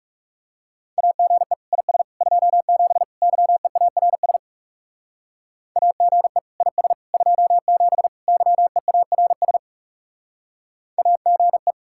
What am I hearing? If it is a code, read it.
Morse code, heard as a keyed tone, about 35 wpm, 700 Hz. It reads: AGE IS 27 YEARS AGE IS 27 YEARS AGE
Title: Speed-Racing Common QSO Phrases - 30wpm